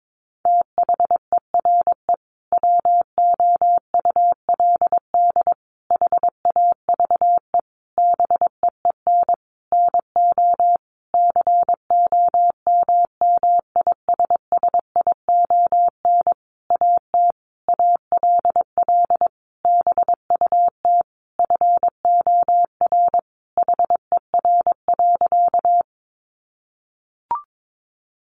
Morse code, heard as a keyed tone, about 22 wpm, 700 Hz. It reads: THERE WOULD HAVE BEEN NO COMMISSION AT ALL BUT FOR HER.
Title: Sentences from Top 1400 Words 22wpm